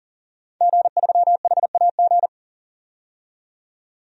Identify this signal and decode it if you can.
Morse code, heard as a keyed tone, about 40 wpm, 700 Hz. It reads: G3HAG
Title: DXCC-100 Call Signs - 5 Characters - 40wpm